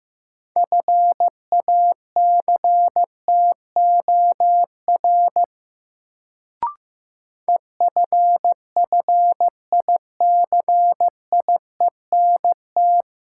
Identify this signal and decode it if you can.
Morse code, heard as a keyed tone, about 15 wpm, 700 Hz. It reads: FACTOR EFFICIENT
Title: Sentences from Top 1300 Words - New Word Drill Alt - 15wpm